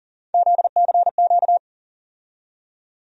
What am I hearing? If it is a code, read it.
Morse code, heard as a keyed tone, about 40 wpm, 700 Hz. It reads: ZCQ
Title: USA Call Sign Postfixes - 3 Letters 40wpm